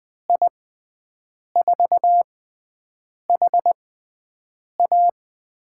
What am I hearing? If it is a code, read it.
Morse code, heard as a keyed tone, about 20 wpm, spaced as for 7 wpm, 700 Hz. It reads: I4HA
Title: USA Call Signs - 6 Characters - 6x Character Spacing - 20wpm